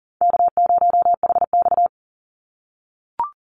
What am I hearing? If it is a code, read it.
Morse code, heard as a keyed tone, about 42 wpm, 700 Hz. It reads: K0HX
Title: Easy Call Signs - 40wpm